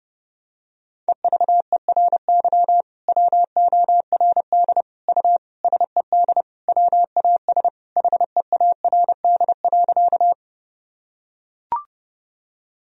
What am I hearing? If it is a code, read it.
Morse code, heard as a keyed tone, about 30 wpm, 700 Hz. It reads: EVERY WORD U SED WAS HEARD.
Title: Sentences from Top 400 Words and Abbreviations 30wpm